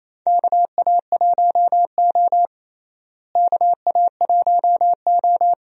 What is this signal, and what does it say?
Morse code, heard as a keyed tone, about 28 wpm, 700 Hz. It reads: KA1O KA1O
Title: USA Call Signs - 4 Characters - Repeated 3x - 28wpm